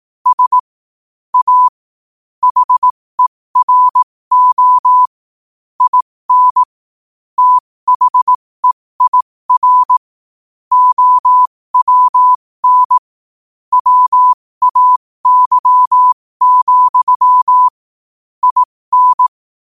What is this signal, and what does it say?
Morse code, heard as a keyed tone, about 18 wpm, spaced as for 15 wpm, 1000 Hz. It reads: S A HERO IN THEIR OWN WAY, IN
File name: qotdfiles / 15wpm